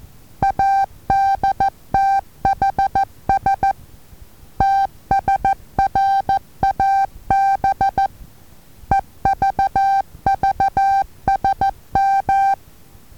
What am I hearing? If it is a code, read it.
Morse code, heard as a keyed tone, about 14 wpm, 780 Hz. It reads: ADTHS TSRAB EVVSM
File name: Taviro HA6VA B